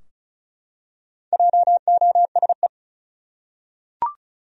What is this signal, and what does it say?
Morse code, heard as a keyed tone, about 34 wpm, 700 Hz. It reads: JOSE
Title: USA Common Names - Max 4 Letters - 35wpm